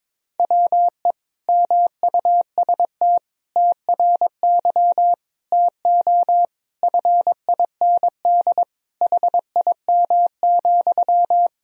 Morse code, heard as a keyed tone, about 22 wpm, 700 Hz. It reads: WE MUST TRY TO FIND HIM,
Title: Sentences from Top 400 Words and Abbreviations 22wpm